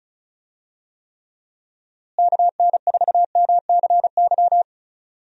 Morse code, heard as a keyed tone, about 34 wpm, 700 Hz. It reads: KN4MCY
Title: USA Call Signs - 6 Characters - Repeated 2x - 35wpm